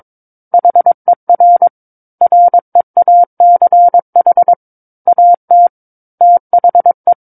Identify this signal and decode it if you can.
Morse code, heard as a keyed tone, about 22 wpm, 700 Hz. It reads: HER REACH AT THE
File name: 06-Chapter-06 22 WPM